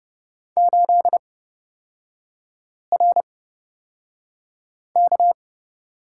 Morse code, heard as a keyed tone, about 30 wpm, spaced as for 5 wpm, 700 Hz. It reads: 8RK
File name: Lesson 37e - Letter x - Three Characters 30wpm